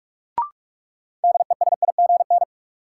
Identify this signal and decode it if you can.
Morse code, heard as a keyed tone, about 46 wpm, 700 Hz. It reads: DESIGN
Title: Sentences from Top 800 Words - New Word Drill - 45wpm